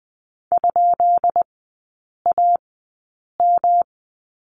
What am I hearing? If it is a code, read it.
Morse code, heard as a keyed tone, about 20 wpm, spaced as for 8 wpm, 700 Hz. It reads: ?AM
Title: 133 Lesson 22e - Question Mark - Three Characters 20wpm